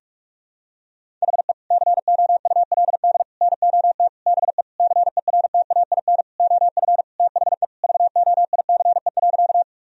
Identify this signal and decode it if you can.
Morse code, heard as a keyed tone, about 46 wpm, 700 Hz. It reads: HE COULD NOT BE CERTAIN OF THE VOICE.
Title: Sentences from Top 1000 Words - Review Part 2 - 45wpm